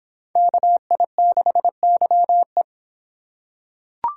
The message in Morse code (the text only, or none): KI6YE